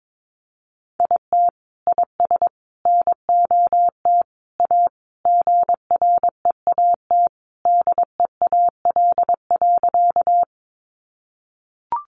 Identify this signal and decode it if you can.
Morse code, heard as a keyed tone, about 22 wpm, 700 Hz. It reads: IT IS NOT A GREAT DEAL.
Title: Sentences from Top 800 Words 22wpm